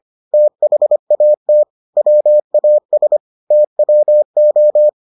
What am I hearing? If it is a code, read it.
Morse code, heard as a keyed tone, about 25 wpm, 600 Hz. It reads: THAT WAS TWO